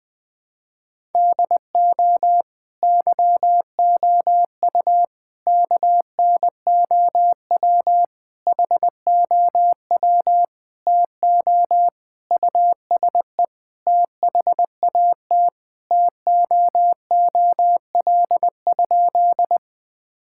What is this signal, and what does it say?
Morse code, heard as a keyed tone, about 20 wpm, 700 Hz. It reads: DO YOU KNOW HOW TO USE THAT TOOL?
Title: Sentences from Top 1000 Words - Review Part 6 - 20wpm